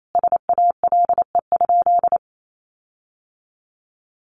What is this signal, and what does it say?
Morse code, heard as a keyed tone, about 28 wpm, 700 Hz. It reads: SALE?
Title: Sentences from Top 1400 Words 28wpm